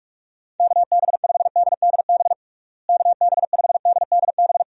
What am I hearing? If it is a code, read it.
Morse code, heard as a keyed tone, about 46 wpm, 700 Hz. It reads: KB5DDB KB5DDB
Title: USA Call Signs - 6 Characters - Repeated 2x - 45wpm